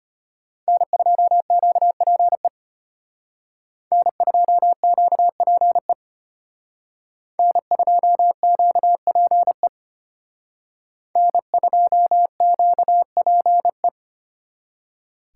Morse code, heard as a keyed tone, about 32 wpm, 700 Hz. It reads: N2QPE N2QPE N2QPE N2QPE